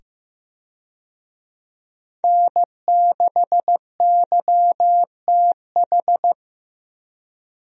Morse code, heard as a keyed tone, about 15 wpm, 700 Hz. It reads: N6YTH
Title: USA Call Signs - 5 Characters - 15wpm